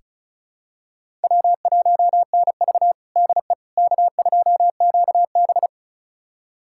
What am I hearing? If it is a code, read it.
Morse code, heard as a keyed tone, about 36 wpm, 700 Hz. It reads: W1NV DE K2QB
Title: USA Callsign DE Callsign - 4 Characters - 35wpm